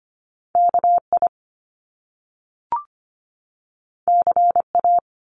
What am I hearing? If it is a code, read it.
Morse code, heard as a keyed tone, about 25 wpm, 700 Hz. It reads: KI CA
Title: Top Two Letter Combinations in Words 25wpm